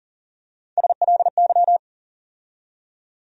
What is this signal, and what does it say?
Morse code, heard as a keyed tone, about 40 wpm, 700 Hz. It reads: SLY
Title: USA Call Sign Postfixes - 3 Letters 40wpm